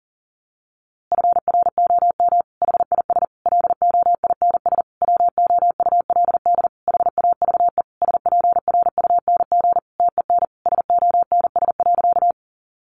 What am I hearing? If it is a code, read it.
Morse code, heard as a keyed tone, about 40 wpm, 700 Hz. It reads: FROM HIS LOINS WOULD HAVE SPRUNG TEN SONS.
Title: A Sound Of Thunder 40wpm